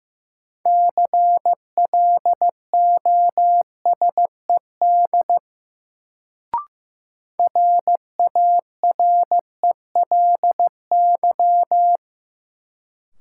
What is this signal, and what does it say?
Morse code, heard as a keyed tone, about 15 wpm, 700 Hz. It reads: CLOSED RARELY
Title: Sentences from Top 1400 Words - New Word Drill Alt - 15wpm